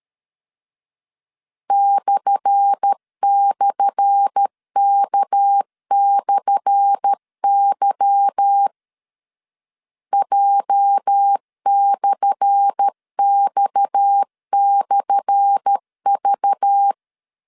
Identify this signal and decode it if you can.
Morse code, heard as a keyed tone, about 13 wpm, 800 Hz. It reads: //K/Y J/X/V
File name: LEZIONI / 10 SIGNS / 10A SLASH 12wpm 2314